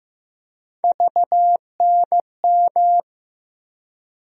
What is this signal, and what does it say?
Morse code, heard as a keyed tone, about 15 wpm, 700 Hz. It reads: VNM
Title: USA Call Sign Postfixes - 3 Letters 15wpm